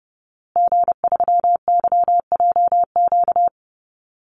Morse code, heard as a keyed tone, about 30 wpm, 700 Hz. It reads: G3YJQ
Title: Call Signs Repeated 3x - 30wpm